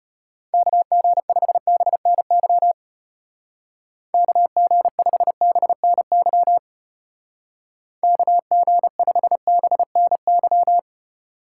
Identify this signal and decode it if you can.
Morse code, heard as a keyed tone, about 30 wpm, 700 Hz. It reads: KG5BNY KG5BNY KG5BNY
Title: Speed Racing - USA 2x3 Call Signs 25wpm